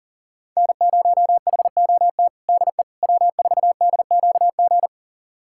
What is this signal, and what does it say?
Morse code, heard as a keyed tone, about 40 wpm, 700 Hz. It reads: N0HOT DE W4DQG
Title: USA Callsign DE Callsign - 5 Characters - 40wpm